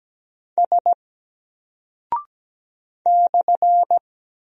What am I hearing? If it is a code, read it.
Morse code, heard as a keyed tone, about 17 wpm, 700 Hz. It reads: S /
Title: Single Character (Letter:Number:Punctuation:Prosign) 17wpm